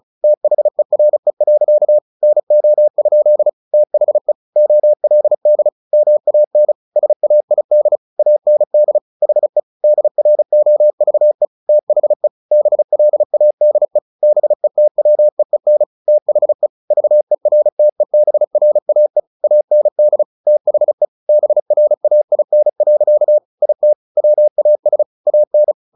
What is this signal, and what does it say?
Morse code, heard as a keyed tone, about 34 wpm, 600 Hz. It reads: THERE. NO? THE OLD MAN SAID AND HE DROVE THE BLADE BETWEEN THE VERTEBRAE AND THE BRAIN. IT WAS AN
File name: the old man and the sea 2 35wpm